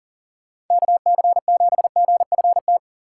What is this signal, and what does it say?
Morse code, heard as a keyed tone, about 51 wpm, 700 Hz. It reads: KC7GFT